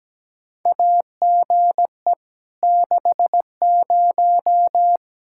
Morse code, heard as a keyed tone, about 17 wpm, 700 Hz. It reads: AGE 60
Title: Practice QSO - Exchange 3 - More Info - 17wpm